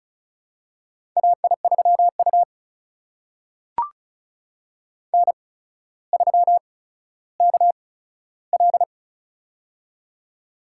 Morse code, heard as a keyed tone, about 36 wpm, spaced as for 15 wpm, 700 Hz. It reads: AI3U N3KL
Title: USA Call Signs - 8x Character Spacing - 35wpm